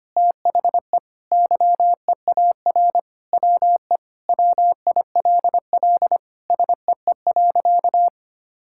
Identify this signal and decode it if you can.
Morse code, heard as a keyed tone, about 25 wpm, 700 Hz. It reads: THE YEAR WE WILL SEE.
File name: practice / 25wpm / Sentences with 3-Word Phrases - 25wpm